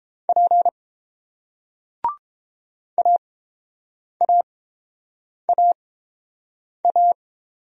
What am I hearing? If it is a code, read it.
Morse code, heard as a keyed tone, about 30 wpm, spaced as for 7 wpm, 700 Hz. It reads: P AAAA